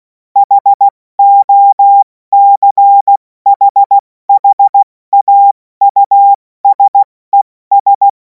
Morse code, heard as a keyed tone, about 16 wpm, 800 Hz. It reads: HOCHHAUSES
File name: Morsekurs / Witze / Witz 11. August 2026 70 F80